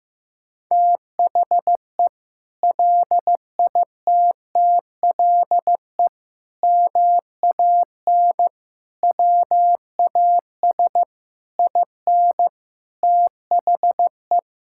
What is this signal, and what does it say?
Morse code, heard as a keyed tone, about 15 wpm, 700 Hz. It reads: THE LITTLE MAN WAS IN THE
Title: Sentences from Top 1600 Words 15wpm